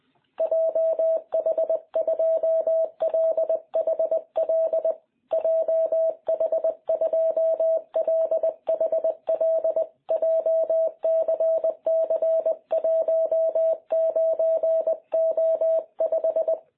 Morse code, heard as a keyed tone, about 20 wpm, 650 Hz. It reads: JH2LHL JH2LHL JCC19O5
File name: jh8cbh / 2021cw